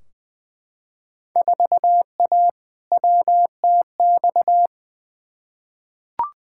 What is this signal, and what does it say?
Morse code, heard as a keyed tone, about 20 wpm, 700 Hz. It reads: 4A WTX
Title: Field Day Exchanges Encore - 20wpm